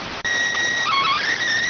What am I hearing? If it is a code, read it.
SSTV (slow-scan television), PD 290.